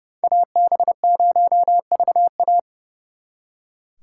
Morse code, heard as a keyed tone, about 38 wpm, 700 Hz. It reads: AB0VA